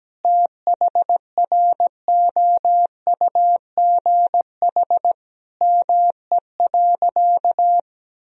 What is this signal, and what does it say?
Morse code, heard as a keyed tone, about 17 wpm, 700 Hz. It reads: THROUGH ME.
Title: Sentences from Top 1000 Words - Review Part 1 - 17wpm